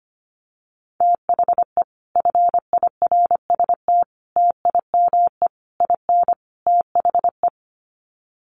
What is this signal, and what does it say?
Morse code, heard as a keyed tone, about 25 wpm, 700 Hz. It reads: THE FIRST TIME IN THE
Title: Sets of 5 Words Encore 25wpm